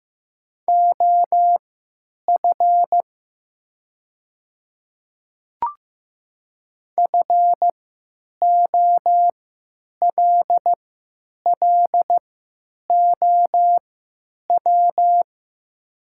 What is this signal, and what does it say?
Morse code, heard as a keyed tone, about 15 wpm, spaced as for 9 wpm, 700 Hz. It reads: OF FOLLOW
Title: 3x Character Spacing - Top 200 Words - 15wpm